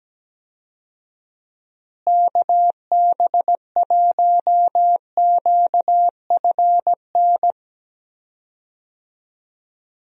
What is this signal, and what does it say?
Morse code, heard as a keyed tone, about 17 wpm, 700 Hz. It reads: KB1QFN
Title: USA Call Signs - 6 Characters - Repeated 2x - 17wpm